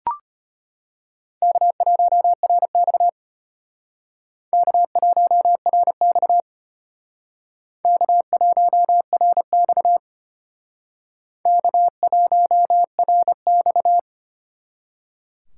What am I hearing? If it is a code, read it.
Morse code, heard as a keyed tone, about 33 wpm, 700 Hz. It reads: K1RX K1RX K1RX K1RX